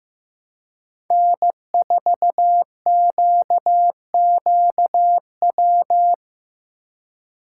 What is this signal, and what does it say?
Morse code, heard as a keyed tone, about 16 wpm, 700 Hz. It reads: N4QQW
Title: USA Call Signs - 5 Characters - 4x Character Spacing - 15wpm